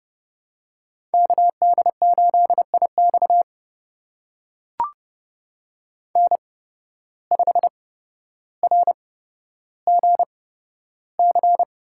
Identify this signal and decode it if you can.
Morse code, heard as a keyed tone, about 30 wpm, spaced as for 13 wpm, 700 Hz. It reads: KD8IX N5RGC
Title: USA Call Signs - 5 Characters - 8x Character Spacing - 30wpm